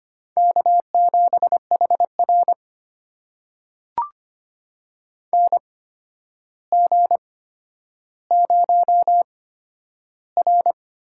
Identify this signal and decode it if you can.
Morse code, heard as a keyed tone, about 25 wpm, spaced as for 11 wpm, 700 Hz. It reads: K7HR NG0R
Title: USA Call Signs - 8x Character Spacing - 25wpm